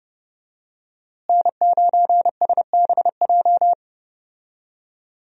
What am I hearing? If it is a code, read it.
Morse code, heard as a keyed tone, about 30 wpm, 700 Hz. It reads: N9SBJ
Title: USA Call Signs - 5 Characters - Repeated 3x - 30wpm